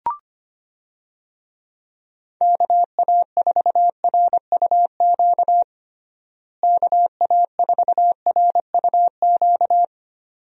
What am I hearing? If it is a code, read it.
Morse code, heard as a keyed tone, about 25 wpm, 700 Hz. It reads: KA4RUQ KA4RUQ
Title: USA Call Signs - 6 Characters - Repeated 2x - 25wpm